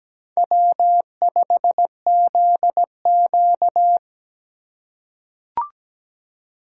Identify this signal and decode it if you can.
Morse code, heard as a keyed tone, about 17 wpm, 700 Hz. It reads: W5ZQ
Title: USA Call Signs - 4 Characters - 3x Character Spacing - 17wpm